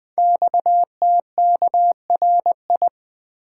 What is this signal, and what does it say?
Morse code, heard as a keyed tone, about 20 wpm, 700 Hz. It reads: XTKRI
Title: Code Groups - 5 Letters - 20wpm